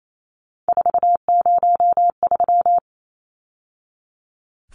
Morse code, heard as a keyed tone, about 28 wpm, 700 Hz. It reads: 403
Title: Numbers - 3-Digit 28wpm